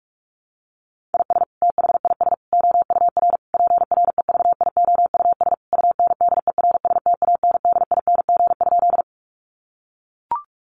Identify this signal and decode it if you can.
Morse code, heard as a keyed tone, about 44 wpm, 700 Hz. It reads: IS THIS OUR PREVIOUS UNDERSTANDING?
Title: Sentences from Top 1500 Words 45wpm